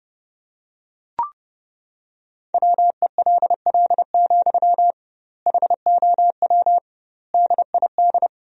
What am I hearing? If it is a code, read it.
Morse code, heard as a keyed tone, about 30 wpm, 700 Hz. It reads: WELL, HOW DID
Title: 2x Word-Spacing - Sentences from Top 200 Words - 30wpm